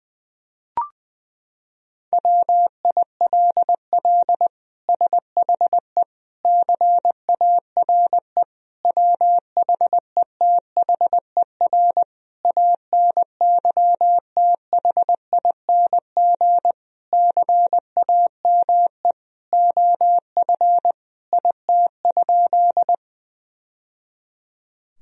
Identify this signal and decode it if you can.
Morse code, heard as a keyed tone, about 20 wpm, 700 Hz. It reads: WILL SHE CARE WHETHER ANYTHING CAME OF IT?
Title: Sentences from Top 1000 Words - Review Part 3 - 20wpm